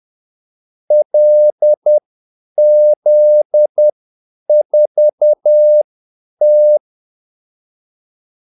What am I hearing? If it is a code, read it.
Morse code, heard as a keyed tone, about 10 wpm, spaced as for 8 wpm, 600 Hz. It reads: LZ4T
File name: Callsigns-10-8wpm